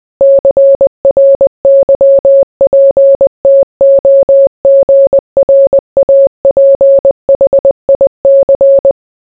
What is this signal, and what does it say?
Morse code, heard as a keyed tone, about 20 wpm, 550 Hz. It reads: CRYPTOGRAPHIC